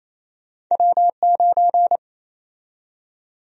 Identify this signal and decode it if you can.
Morse code, heard as a keyed tone, about 28 wpm, 700 Hz. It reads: W9
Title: USA Call Sign Prefixes - Repeated 2x - 28wpm